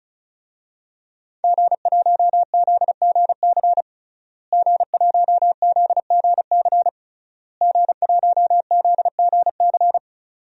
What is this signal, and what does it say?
Morse code, heard as a keyed tone, about 36 wpm, 700 Hz. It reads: G1ZGC G1ZGC G1ZGC
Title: DXCC-100 Call Signs - 5 Characters - Repeated 3x - 35wpm